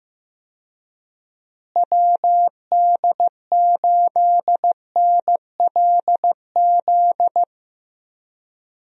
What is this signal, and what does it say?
Morse code, heard as a keyed tone, about 16 wpm, 700 Hz. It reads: WD8NLZ
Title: USA Call Signs - 6 Characters - Repeated 2x - 15wpm